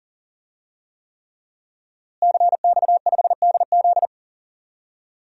Morse code, heard as a keyed tone, about 40 wpm, 700 Hz. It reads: CX5DZ
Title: DXCC-100 Call Signs - 5 Characters - Repeated 3x - 40wpm